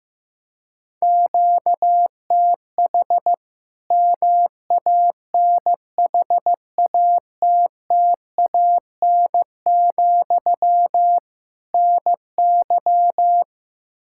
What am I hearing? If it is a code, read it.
Morse code, heard as a keyed tone, about 16 wpm, 700 Hz. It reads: QTH MANHATTAN, NY
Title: Practice QSO - Exchange 1 - Beginning - 15wpm